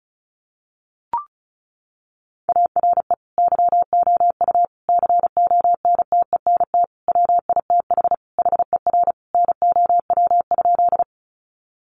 Morse code, heard as a keyed tone, about 34 wpm, 700 Hz. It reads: ARE YOU CONTENT WITH HER NOW?
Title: Sentences from Top 900 Words 35wpm